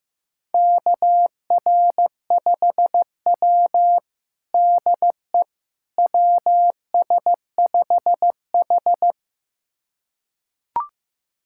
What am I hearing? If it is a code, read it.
Morse code, heard as a keyed tone, about 15 wpm, 700 Hz. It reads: KR5W DE WS5H
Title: USA Callsign DE Callsign - 4 Characters - 15wpm